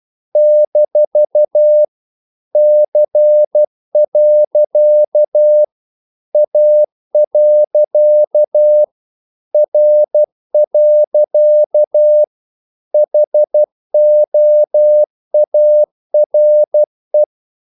Morse code, heard as a keyed tone, about 12 wpm, 600 Hz. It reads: - C. A. R. HOARE
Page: English quotations in CW (telegraphy - morse)